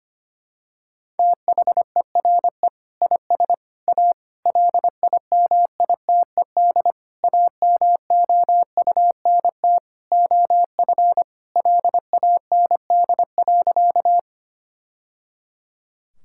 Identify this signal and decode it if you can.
Morse code, heard as a keyed tone, about 25 wpm, 700 Hz. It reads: THERE IS A LIMITED AMOUNT OF LAND.
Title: Sentences from Top 1000 Words - Review Part 6 - 25wpm